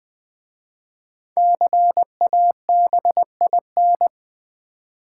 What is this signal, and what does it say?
Morse code, heard as a keyed tone, about 20 wpm, 700 Hz. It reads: CABIN
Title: 5 Letter Words - First 3 Letters Given - 20wpm